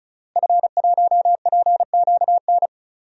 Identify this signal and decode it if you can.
Morse code, heard as a keyed tone, about 36 wpm, 700 Hz. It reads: F1PQN
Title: DXCC-100 Call Signs - 5 Characters - Repeated 3x - 35wpm